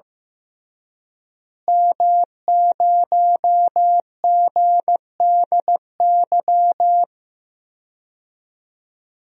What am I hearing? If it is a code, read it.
Morse code, heard as a keyed tone, about 15 wpm, 700 Hz. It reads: M0GDY
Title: DXCC-100 Call Signs - 5 Characters - Repeated 3x - 15wpm